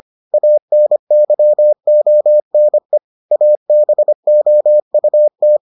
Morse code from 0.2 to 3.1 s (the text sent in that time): ANYONE